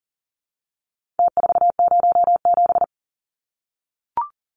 Morse code, heard as a keyed tone, about 40 wpm, 700 Hz. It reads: T407
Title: Numbers - Contest Style - 4-Digit 40wpm